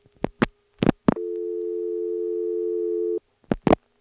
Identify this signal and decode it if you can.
Dial tone (North America), 350 + 440 Hz.